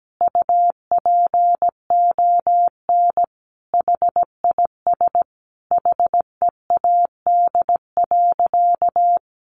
Morse code, heard as a keyed tone, about 17 wpm, 700 Hz. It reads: UPON HIS HEAD.
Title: Sentences from Top 1100 Words 17wpm